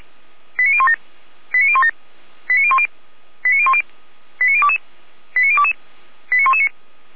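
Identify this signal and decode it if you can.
Five-tone selective call (ZVEI), heard as quick runs of five tones, tones of about 70 ms. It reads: 89017 89017 89010 89010 89022 89022 89119
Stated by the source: ZVEI five-tone selective call received via SDR.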